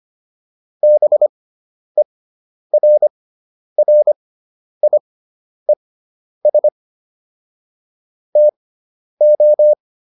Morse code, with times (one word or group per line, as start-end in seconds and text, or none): 0.8-6.7 BERRIES
8.4-9.7 TO